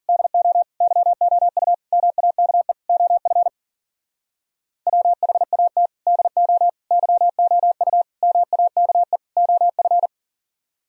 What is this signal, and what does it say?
Morse code, heard as a keyed tone, about 52 wpm, 700 Hz. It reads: DO YOU MAKE OF WHAT DO YOU MAKE OF